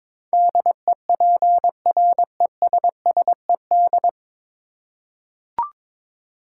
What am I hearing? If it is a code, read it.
Morse code, heard as a keyed tone, about 22 wpm, 700 Hz. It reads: DEPRESSED
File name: practice / 22wpm / 9 Letter Words - First 3 Letters Given - 22wpm